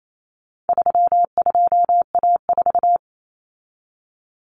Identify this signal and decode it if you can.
Morse code, heard as a keyed tone, about 28 wpm, 700 Hz. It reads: 32A4
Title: Common Cut Numbers - 4-Digit 28wpm